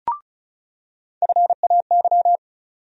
Morse code, heard as a keyed tone, about 36 wpm, 700 Hz. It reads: FAY
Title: USA Call Sign Postfixes - 3 Letters 35wpm